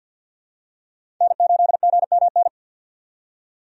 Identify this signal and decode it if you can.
Morse code, heard as a keyed tone, about 51 wpm, 700 Hz. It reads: N8GMN
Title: USA Call Signs - 5 Characters - 50wpm